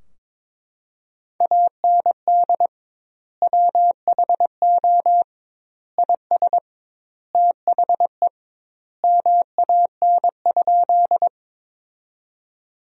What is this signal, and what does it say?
Morse code, heard as a keyed tone, about 22 wpm, 700 Hz. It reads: AND WHO IS THE MAN?
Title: 2x Word-Spacing - Sentences from Top 200 Words - 22wpm